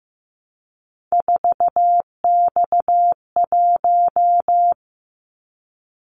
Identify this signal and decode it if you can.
Morse code, heard as a keyed tone, about 15 wpm, 700 Hz. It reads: 4X1
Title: Callsign Prefixes for Top 50 DXCC - 15wpm